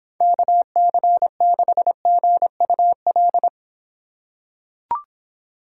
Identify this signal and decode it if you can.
Morse code, heard as a keyed tone, about 24 wpm, 700 Hz. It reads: KC6GUL